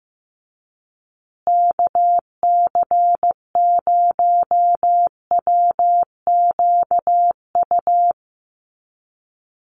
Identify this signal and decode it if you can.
Morse code, heard as a keyed tone, about 15 wpm, 700 Hz. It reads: KC0WQU
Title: USA Call Signs Repeated 2x 15wpm